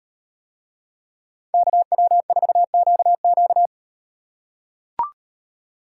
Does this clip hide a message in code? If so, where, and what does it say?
Yes: Morse code from 1.5 to 3.7 s, heard as a keyed tone, about 34 wpm, 700 Hz. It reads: KW4QQ